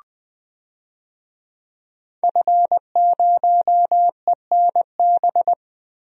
Morse code, heard as a keyed tone, about 20 wpm, 700 Hz. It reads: F0ENB
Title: DXCC-100 Call Signs - 5 Characters - Repeated 3x - 20wpm